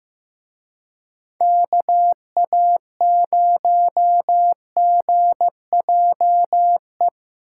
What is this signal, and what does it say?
Morse code, heard as a keyed tone, about 17 wpm, 700 Hz. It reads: KA0GJE
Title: Speed Racing - USA Call Signs 15wpm